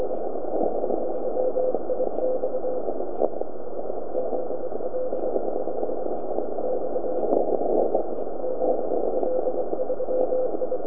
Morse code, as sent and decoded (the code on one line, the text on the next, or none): none